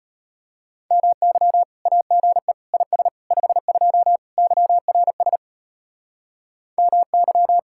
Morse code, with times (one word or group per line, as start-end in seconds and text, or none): 0.9-1.6 MY
1.9-2.5 AGE
2.7-3.1 IS
3.3-4.2 52
4.4-5.4 YRS
6.8-7.6 MY